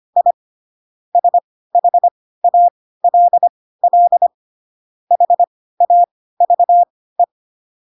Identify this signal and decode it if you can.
Morse code, heard as a keyed tone, about 25 wpm, spaced as for 16 wpm, 700 Hz. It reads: I SHALL HAVE
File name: Pinocchio / Pinocchio ch7 10 25 ik1hge